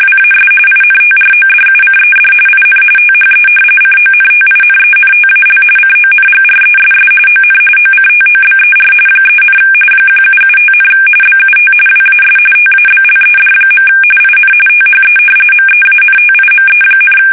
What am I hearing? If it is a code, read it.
RTTY, about 45.45 baud. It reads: WAS INTRODUCED IN THE CONTINENTAL CONGRESS ON OCTOBER 3, 1775, BUT WAS TABLED. IN THE MEANTIME,
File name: fsk850 / usnavy1